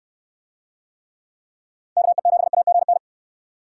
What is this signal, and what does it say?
Morse code, heard as a keyed tone, about 69 wpm, 700 Hz. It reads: KE7AGN